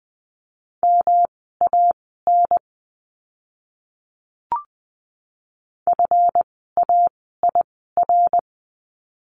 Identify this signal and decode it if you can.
Morse code, heard as a keyed tone, about 20 wpm, spaced as for 14 wpm, 700 Hz. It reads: MAN FAIR
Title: 139 Lesson 23f - Letter f - Words 20wpm